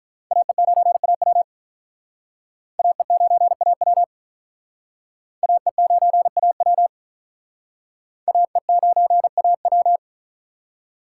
Morse code, heard as a keyed tone, about 46 wpm, 700 Hz. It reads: AE9AW AE9AW AE9AW AE9AW